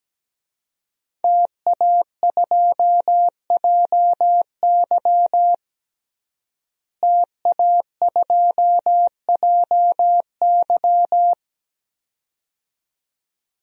Morse code, heard as a keyed tone, about 17 wpm, 700 Hz. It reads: TA2JY TA2JY
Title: DXCC-100 Call Signs - 5 Characters - Repeated 2x - 17wpm